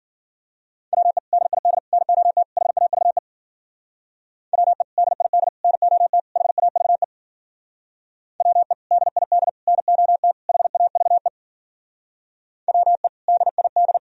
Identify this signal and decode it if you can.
Morse code, heard as a keyed tone, about 52 wpm, 700 Hz. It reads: WE DID NOT HAVE WE DID NOT HAVE WE DID NOT HAVE WE DID